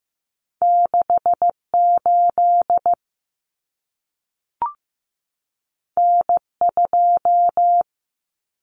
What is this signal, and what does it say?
Morse code, heard as a keyed tone, about 15 wpm, 700 Hz. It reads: 68 N2
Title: Common Cut Numbers - 2-Digit 15wpm